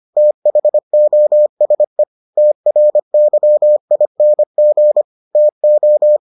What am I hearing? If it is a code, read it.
Morse code, heard as a keyed tone, about 25 wpm, 600 Hz. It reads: THOSE TRYING TO